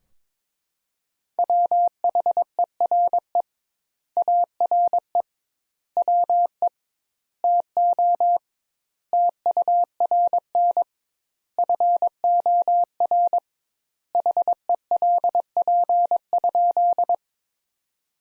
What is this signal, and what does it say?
Morse code, heard as a keyed tone, about 22 wpm, 700 Hz. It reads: WHERE ARE WE TO TURN FOR HELP?
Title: Sentences Top 200 Words - 22wpm